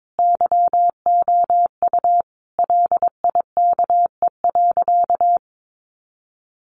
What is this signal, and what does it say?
Morse code, heard as a keyed tone, about 22 wpm, 700 Hz. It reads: YOU LIKE.
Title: Sentences from Top 1200 Words 22wpm